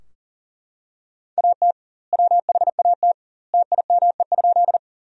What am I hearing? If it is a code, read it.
Morse code, heard as a keyed tone, about 40 wpm, 700 Hz. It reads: AT WHAT TIME?
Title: 2x Word-Spacing - Sentences from Top 100 Words - 40wpm